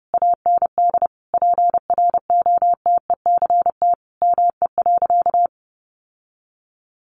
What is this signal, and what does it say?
Morse code, heard as a keyed tone, about 30 wpm, 700 Hz. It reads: AND PROTECT ME.
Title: Sentences from Top 1100 Words 30wpm